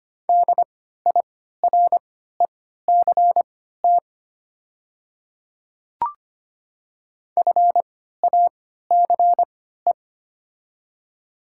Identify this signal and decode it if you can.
Morse code, heard as a keyed tone, about 25 wpm, spaced as for 14 wpm, 700 Hz. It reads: DIRECT FACE
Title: 3x Character Spacing - Top 500 Words - 25wpm